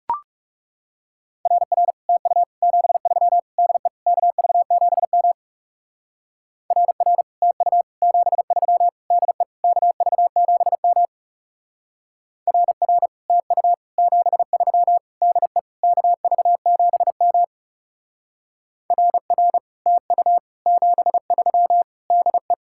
Morse code, heard as a keyed tone, about 37 wpm, 700 Hz. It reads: RR TU 73 DE KV7M RR TU 73 DE KV7M RR TU 73 DE KV7M RR TU 73 DE
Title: Speed-Racing Common QSO Phrases Encore - 30wpm